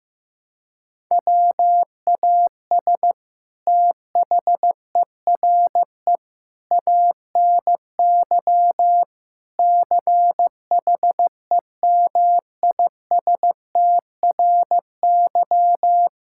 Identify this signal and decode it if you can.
Morse code, heard as a keyed tone, about 15 wpm, 700 Hz. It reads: WAS THERE ANY CHEMISTRY
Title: Sentences from Top 1000 Words - Review Part 6 - 15wpm